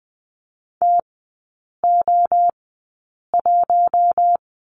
Morse code, heard as a keyed tone, about 20 wpm, spaced as for 8 wpm, 700 Hz. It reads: TO1
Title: 047 Lesson 9e - Number 4 - Three Characters 20wpm